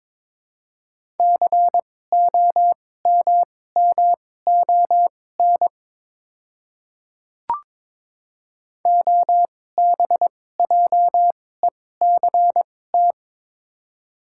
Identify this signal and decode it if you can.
Morse code, heard as a keyed tone, about 21 wpm, spaced as for 16 wpm, 700 Hz. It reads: COMMON OBJECT
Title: 2x Character Spacing - Top 500 Words - 22wpm